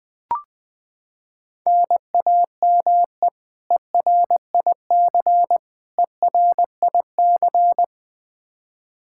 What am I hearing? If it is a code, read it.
Morse code, heard as a keyed tone, about 20 wpm, 700 Hz. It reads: NAME ERIC ERIC
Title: Practice QSO - Exchange 1 - Beginning - 20wpm